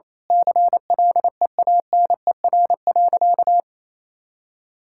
Morse code, heard as a keyed tone, about 27 wpm, 700 Hz. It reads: CLEANER.
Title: Pangram Challenge v2 - 28wpm